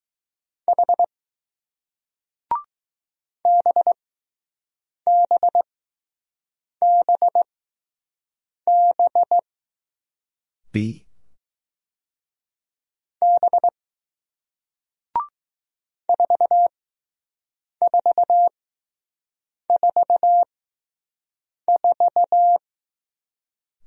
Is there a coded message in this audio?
Yes: Morse code from 0.7 to 22.6 s, heard as a keyed tone, about 20 wpm, spaced as for 6 wpm, 700 Hz. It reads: H BBBB B 4444